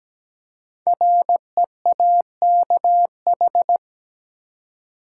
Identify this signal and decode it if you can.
Morse code, heard as a keyed tone, about 17 wpm, 700 Hz. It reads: REAKH
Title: Code Groups - 5 Letter:Numbers - 17wpm